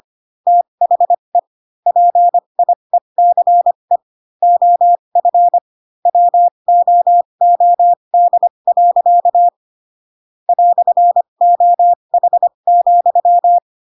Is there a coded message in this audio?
Yes: Morse code from 0.5 to 13.6 s, heard as a keyed tone, about 25 wpm, spaced as for 22 wpm, 700 Hz. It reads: THE PIECE OF WOOD. "OH,